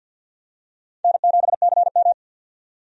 Morse code, heard as a keyed tone, about 53 wpm, 700 Hz. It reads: N7CM
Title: USA Call Signs - 4 Characters - 50wpm